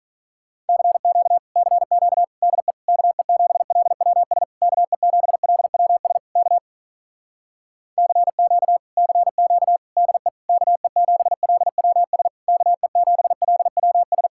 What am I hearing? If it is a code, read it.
Morse code, heard as a keyed tone, about 46 wpm, 700 Hz. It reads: CQ CQ DE KE7LWS KE7LWS K CQ CQ DE KE7LWS KE7LWS